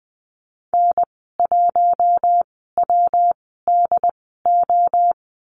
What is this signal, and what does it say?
Morse code, heard as a keyed tone, about 20 wpm, spaced as for 14 wpm, 700 Hz. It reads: N1WDO
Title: 161 Lesson 26g - Letter g - Call Signs 20wpm